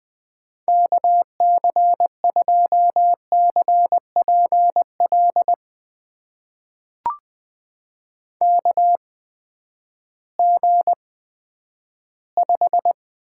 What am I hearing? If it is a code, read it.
Morse code, heard as a keyed tone, about 20 wpm, spaced as for 11 wpm, 700 Hz. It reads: KC2CPL KG5
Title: USA Call Signs - 6 Characters - 8x Character Spacing - 20wpm